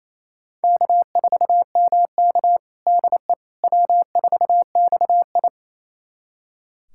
Morse code, heard as a keyed tone, about 28 wpm, 700 Hz. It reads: K4MK DE W4XI
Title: USA Callsign DE Callsign - 4 Characters - 28wpm